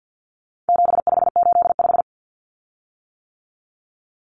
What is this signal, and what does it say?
Morse code, heard as a keyed tone, about 50 wpm, 700 Hz. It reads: 7585
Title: Common Cut Numbers - 4-Digit 50wpm